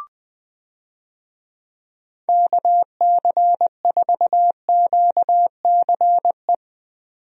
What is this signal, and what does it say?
Morse code, heard as a keyed tone, about 20 wpm, 700 Hz. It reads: KC4QCE
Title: USA Call Signs - 6 Characters - 20wpm